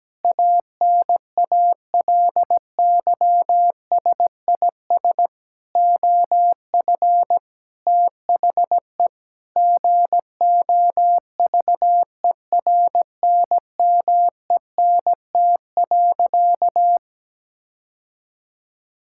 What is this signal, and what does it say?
Morse code, heard as a keyed tone, about 17 wpm, 700 Hz. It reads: ANALYSIS OF THE GOVERNMENT.
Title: Sentences from Top 1000 Words - Review Part 5 - 17wpm